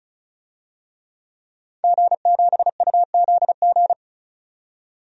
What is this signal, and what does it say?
Morse code, heard as a keyed tone, about 34 wpm, 700 Hz. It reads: G7UZG
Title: DXCC-100 Call Signs - 5 Characters - 35wpm